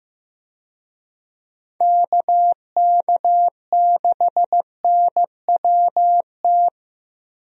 Morse code, heard as a keyed tone, about 15 wpm, 700 Hz. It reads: KK6NWT
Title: USA Call Signs - 6 Characters - Repeated 2x - 15wpm